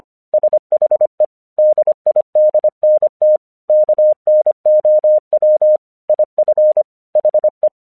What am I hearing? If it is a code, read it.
Morse code, heard as a keyed tone, about 24 wpm, 610 Hz. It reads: SHE DIDNT KNOW IF HE
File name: 1 English 25wpm / AQC17